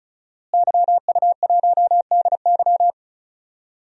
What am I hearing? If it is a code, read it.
Morse code, heard as a keyed tone, about 36 wpm, 700 Hz. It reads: YU1DY
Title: DXCC-100 Call Signs - 5 Characters - Repeated 3x - 35wpm